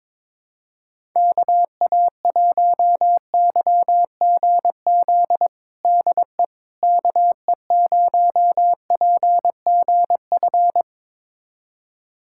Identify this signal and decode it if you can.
Morse code, heard as a keyed tone, about 22 wpm, 700 Hz. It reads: KA1YGZ DE KE0PGF
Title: USA Callsign DE Callsign - 6 Characters - 22wpm